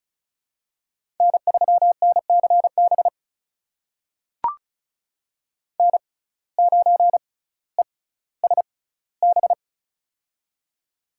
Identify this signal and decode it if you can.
Morse code, heard as a keyed tone, about 34 wpm, spaced as for 18 wpm, 700 Hz. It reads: N3NCB N9ESB
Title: USA Call Signs - 5 Characters - 6x Character Spacing - 35wpm